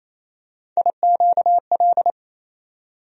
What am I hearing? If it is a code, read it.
Morse code, heard as a keyed tone, about 28 wpm, 700 Hz. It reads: IQL
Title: USA Call Sign Postfixes - 3 Letters 28wpm